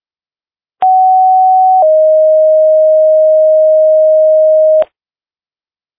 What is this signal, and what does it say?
Two-tone page, heard as a one-second tone, then a three-second tone, about 750 then 620 Hz. It is heard from 0.8 to 4.8 s.